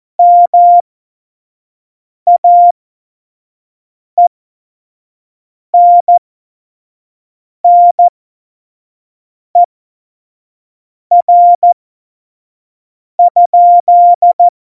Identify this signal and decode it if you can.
Morse code, heard as a keyed tone, about 14 wpm, spaced as for 5 wpm, 700 Hz. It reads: MAENNER?
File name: Morsekurs / Witze / Witz 11. Februar 2020, 25 F70